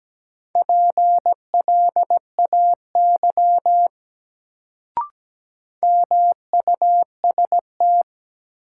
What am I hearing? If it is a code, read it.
Morse code, heard as a keyed tone, about 17 wpm, 700 Hz. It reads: PLAY MUST
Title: Sentences from Top 200 Words - New Word Drill - 17wpm